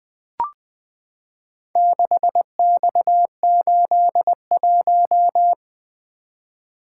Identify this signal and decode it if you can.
Morse code, heard as a keyed tone, about 20 wpm, 700 Hz. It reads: 6X81
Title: Four Letter-Number Combinations - 20wpm